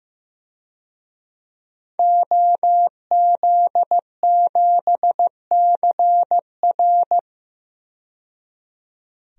Morse code, heard as a keyed tone, about 15 wpm, 700 Hz. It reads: OZ7CR
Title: DXCC-100 Call Signs - 5 Characters - 15wpm